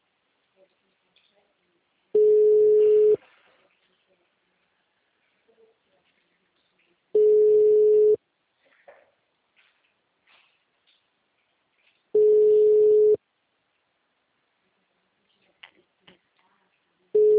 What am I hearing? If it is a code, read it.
Ringback tone (Europe, 425 Hz), with speech.
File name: out- -2028- - - .103